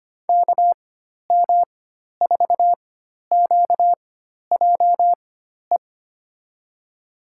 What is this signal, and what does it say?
Morse code, heard as a keyed tone, about 25 wpm, spaced as for 12 wpm, 700 Hz. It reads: KM4QJE
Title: USA Call Signs - 6 Characters - 4x Character Spacing - 25wpm